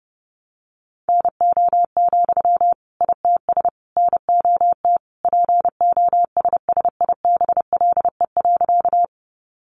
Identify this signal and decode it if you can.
Morse code, heard as a keyed tone, about 30 wpm, 700 Hz. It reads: NO, ITS NOT POSSIBLE.
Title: Sentences from Top 600 Words 30wpm